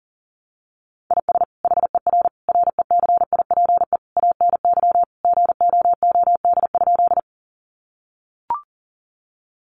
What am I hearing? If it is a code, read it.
Morse code, heard as a keyed tone, about 40 wpm, 700 Hz. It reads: IS HER RECIPE ANY GOOD?
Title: Sentences from Top 1200 Words 40wpm